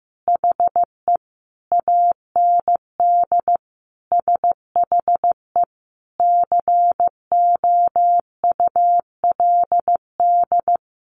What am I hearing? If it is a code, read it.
Morse code, heard as a keyed tone, about 15 wpm, 700 Hz. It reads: HE AND SHE COULD
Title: Sentences from Top 1400 Words 15wpm